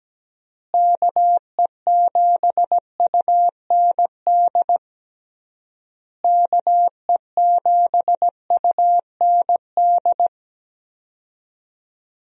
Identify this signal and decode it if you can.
Morse code, heard as a keyed tone, about 17 wpm, 700 Hz. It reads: KE7UND KE7UND
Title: USA Call Signs - 6 Characters - Repeated 3x - 17wpm